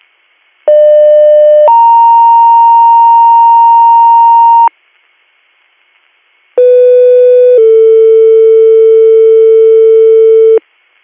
Two-tone page, heard as a one-second tone, then a three-second tone, about 600 then 930 Hz, about 500 then 450 Hz. It is heard from 0.7 to 10.6 s.